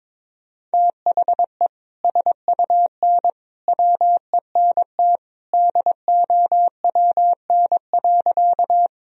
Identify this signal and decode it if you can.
Morse code, heard as a keyed tone, about 22 wpm, 700 Hz. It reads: THE SUN WENT DOWN.
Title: Sentences from Top 1000 Words - Review Part 1 - 22wpm